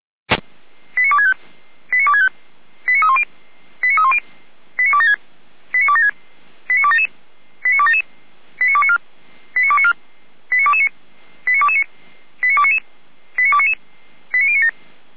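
Five-tone selective call (ZVEI), heard as quick runs of five tones, tones of about 80 ms. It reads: 89265 89265 89310 89310 89276 89276 89277 89277 89294 89294 89229 89229 89299 89299 89097
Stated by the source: ZVEI five-tone call 89174 received by SDR.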